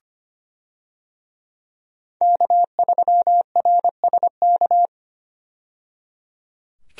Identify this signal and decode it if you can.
Morse code, heard as a keyed tone, about 25 wpm, 700 Hz. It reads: K3RSK
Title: USA Call Signs - 5 Characters - 25wpm